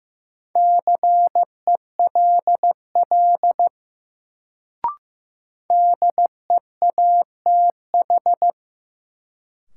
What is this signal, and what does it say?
Morse code, heard as a keyed tone, about 15 wpm, 700 Hz. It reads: CELL DEATH
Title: Sentences from Top 1100 Words - New Word Drill Alt - 15wpm